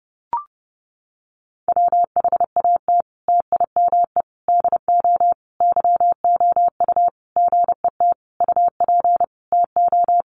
Morse code, heard as a keyed tone, about 30 wpm, 700 Hz. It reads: WHAT TIME DO YOU GET UP TO
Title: Sentences from Top 1500 Words 30wpm